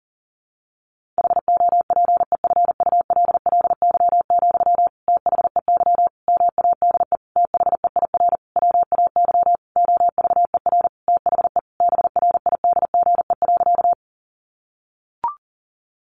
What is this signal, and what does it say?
Morse code, heard as a keyed tone, about 40 wpm, 700 Hz. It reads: HOPEFULLY, THEY MADE THEIR WAY OVER THE BRIDGE.
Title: Sentences from Top 1500 Words 40wpm